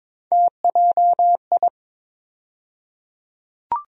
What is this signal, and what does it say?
Morse code, heard as a keyed tone, about 22 wpm, 700 Hz. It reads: TJI
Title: USA Call Sign Postfixes - 3 Letters 22wpm